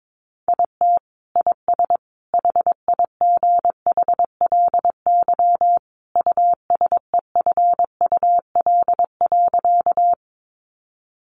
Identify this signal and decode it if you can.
Morse code, heard as a keyed tone, about 22 wpm, 700 Hz. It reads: IT IS HIGHLY USEFUL.
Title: Sentences from Top 1100 Words 22wpm